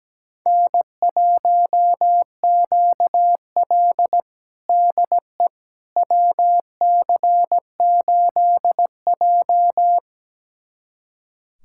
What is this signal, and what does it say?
Morse code, heard as a keyed tone, about 17 wpm, 700 Hz. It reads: N1QL DE WC8J
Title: USA Callsign DE Callsign - 4 Characters - 17wpm